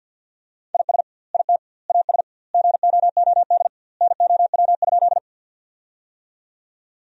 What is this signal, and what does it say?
Morse code, heard as a keyed tone, about 50 wpm, 700 Hz. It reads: IS IT AS GOOD NOW?
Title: 2x Word-Spacing - Sentences from Top 200 Words - 50wpm